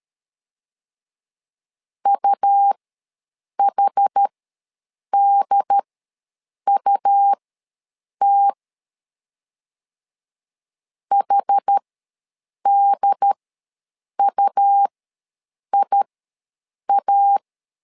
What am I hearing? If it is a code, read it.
Morse code, heard as a keyed tone, about 13 wpm, spaced as for 7 wpm, 800 Hz. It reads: UHDUT HDUIA
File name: LEZIONI / 2 UHDN / 2 UHD ws 12wpm 1722